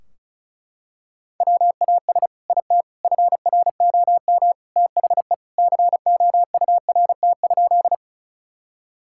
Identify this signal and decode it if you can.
Morse code, heard as a keyed tone, about 34 wpm, 700 Hz. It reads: WAS IT FROM THE COURT?
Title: Sentences from Top 1600 Words 35wpm